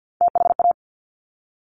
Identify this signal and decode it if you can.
Morse code, heard as a keyed tone, about 50 wpm, 700 Hz. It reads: THA
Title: Top Three Letter Combinations in Words 50wpm